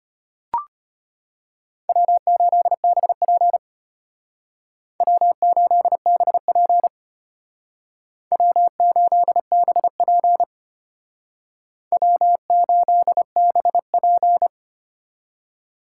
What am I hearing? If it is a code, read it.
Morse code, heard as a keyed tone, about 32 wpm, 700 Hz. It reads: W8BP W8BP W8BP W8BP